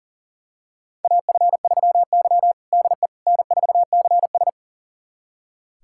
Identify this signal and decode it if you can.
Morse code, heard as a keyed tone, about 40 wpm, 700 Hz. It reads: AF3Y DE N4CS
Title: USA Callsign DE Callsign - 4 Characters - 40wpm